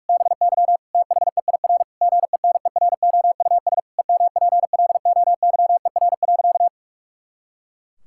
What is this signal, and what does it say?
Morse code, heard as a keyed tone, about 41 wpm, 700 Hz. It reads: BY THEIR GENEROUS EMPLOYER.
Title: Pangram Challenge v2 - 45wpm